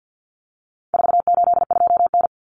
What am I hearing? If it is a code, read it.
Morse code, heard as a keyed tone, about 50 wpm, 700 Hz. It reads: 482N
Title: Common Cut Numbers - 4-Digit 50wpm